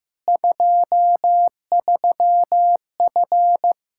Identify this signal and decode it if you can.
Morse code, heard as a keyed tone, about 15 wpm, 700 Hz. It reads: 23F
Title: K1USN SST QSO Practice 15wpm